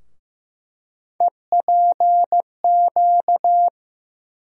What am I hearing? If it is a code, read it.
Morse code, heard as a keyed tone, about 15 wpm, 700 Hz. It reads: EPQ
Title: USA Call Sign Postfixes - 3 Letters 15wpm